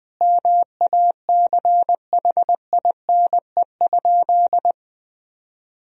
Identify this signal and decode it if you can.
Morse code, heard as a keyed tone, about 21 wpm, 700 Hz. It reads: MACHINE?
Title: Sentences from Top 1000 Words - Review Part 2 - 20wpm